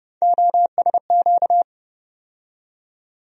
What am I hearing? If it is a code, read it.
Morse code, heard as a keyed tone, about 30 wpm, 700 Hz. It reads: OSQ
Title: USA Call Sign Postfixes - 3 Letters 30wpm